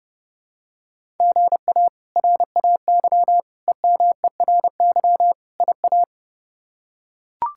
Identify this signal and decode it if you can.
Morse code, heard as a keyed tone, about 30 wpm, 700 Hz. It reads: GA RAY EMERY IA
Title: K1USN SST QSO Practice - 30wpm